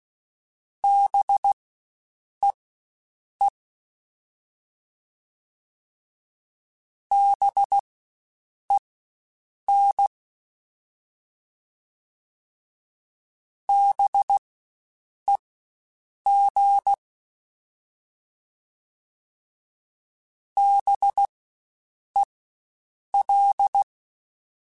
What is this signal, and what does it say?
Morse code, heard as a keyed tone, about 16 wpm, spaced as for 7 wpm, 800 Hz. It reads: BEE BEN BEG BEL
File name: Spaced Alphas / Spaced Alphas mp3s / Bs sp @15